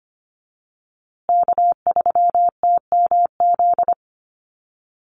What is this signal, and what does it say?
Morse code, heard as a keyed tone, about 25 wpm, 700 Hz. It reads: K3TMZ
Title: USA Call Signs Repeated 2x 25wpm